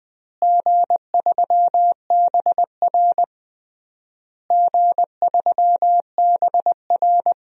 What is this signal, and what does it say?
Morse code, heard as a keyed tone, about 20 wpm, 700 Hz. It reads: G3BR G3BR
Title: DXCC-100 Call Signs - 4 Characters - Repeated 3x - 20wpm